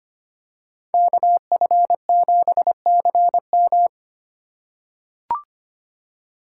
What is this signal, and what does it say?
Morse code, heard as a keyed tone, about 25 wpm, 700 Hz. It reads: KF7CM
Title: USA Call Signs - 5 Characters - 6x Character Spacing - 25wpm